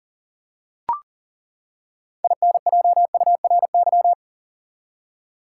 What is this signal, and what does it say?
Morse code, heard as a keyed tone, about 40 wpm, 700 Hz. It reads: INJURY
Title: Six Letter Words - 40wpm